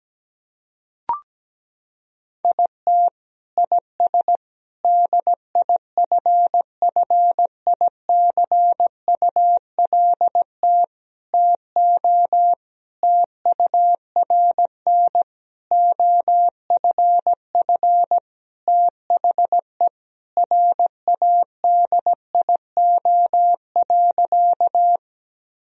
Morse code, heard as a keyed tone, about 17 wpm, 700 Hz. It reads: IT IS DIFFICULT TO TURN OFF THE RADIO.
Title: Sentences from Top 1000 Words - Review Part 3 - 17wpm